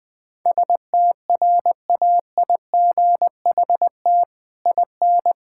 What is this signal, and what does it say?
Morse code, heard as a keyed tone, about 21 wpm, 700 Hz. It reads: STRAIGHT IN
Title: Sentences from Top 1000 Words - Review Part 6 - 20wpm